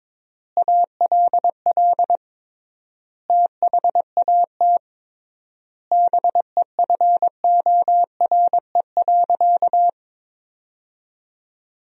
Morse code, heard as a keyed tone, about 22 wpm, 700 Hz. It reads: ALL THAT BEFORE.
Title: 3x Word-Spacing - Sentences from Top 400 Words - 22wpm